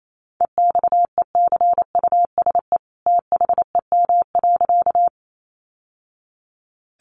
Morse code, heard as a keyed tone, about 28 wpm, 700 Hz. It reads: EXECUSE THEM.
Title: Sentences from Top 700 Words 28wpm